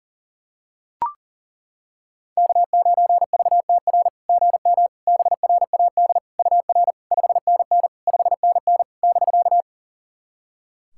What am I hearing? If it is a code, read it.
Morse code, heard as a keyed tone, about 40 wpm, 690 Hz. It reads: K9VTR GM BRAD UR 5NN 5NN <BK>
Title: SOTA QSO Practice 40wpm